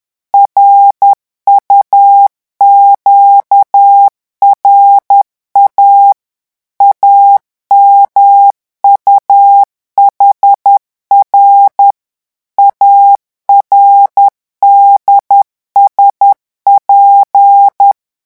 Morse code, heard as a keyed tone, about 10 wpm, 800 Hz. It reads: RUQRA AMUHR ARDSP
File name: morse mp3 / revision5-vitesse 10 mots